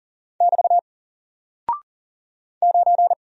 Morse code, heard as a keyed tone, about 38 wpm, 700 Hz. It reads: <BT> 9
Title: Single Character (Letter:Number:Punctuation:Prosign) 40wpm